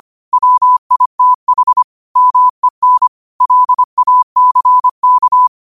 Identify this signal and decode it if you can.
Morse code, heard as a keyed tone, about 25 wpm, 1000 Hz. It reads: WITH MEN LACK
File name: qotdfiles / 25wpm